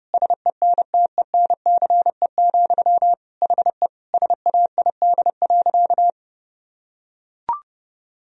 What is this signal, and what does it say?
Morse code, heard as a keyed tone, about 32 wpm, 700 Hz. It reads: SENTENCE, HE SAID.
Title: Sentences from Top 1000 Words - Review Part 1 - 30wpm